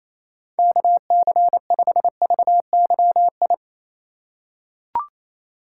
Morse code, heard as a keyed tone, about 28 wpm, 700 Hz. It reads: KC5VYI
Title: USA Call Signs - 6 Characters - 2x Character Spacing - 28wpm